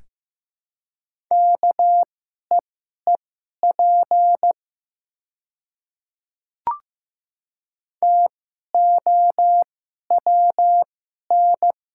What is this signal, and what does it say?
Morse code, heard as a keyed tone, about 15 wpm, spaced as for 11 wpm, 700 Hz. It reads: KEEP TOWN
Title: 2x Character Spacing - Top 500 Words - 15wpm